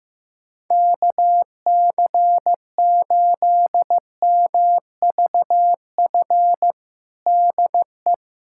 Morse code, heard as a keyed tone, about 15 wpm, 700 Hz. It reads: KC8MVF DE
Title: USA Callsign DE Callsign 15wpm